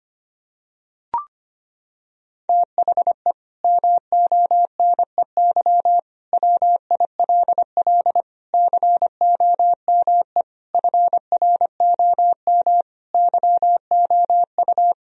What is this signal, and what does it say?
Morse code, heard as a keyed tone, about 25 wpm, 700 Hz. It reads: THE MONEY WILL COME FROM YOU
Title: Sentences from Top 1000 Words - Review Part 4 - 25wpm